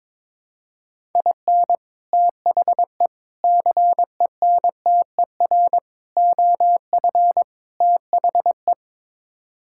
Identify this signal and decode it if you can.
Morse code, heard as a keyed tone, about 22 wpm, 700 Hz. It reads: IN THE CENTER OF THE
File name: practice / 22wpm / Build A Phrase - 5-Word Phrases - 22wpm